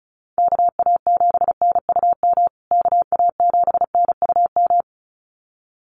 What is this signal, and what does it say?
Morse code, heard as a keyed tone, about 34 wpm, 700 Hz. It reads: KA7NUM KA7NUM
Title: USA Call Signs Repeated 2x 35wpm